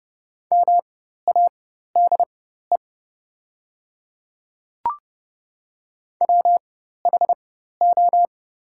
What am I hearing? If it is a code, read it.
Morse code, heard as a keyed tone, about 30 wpm, spaced as for 14 wpm, 700 Hz. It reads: MADE WHO
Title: 4x Character Spacing - Top 200 Words - 30wpm